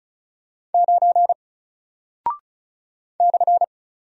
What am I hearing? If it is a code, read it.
Morse code, heard as a keyed tone, about 36 wpm, 700 Hz. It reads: 9 /
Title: Single Character (Letter:Number:Punctuation:Prosign) 35wpm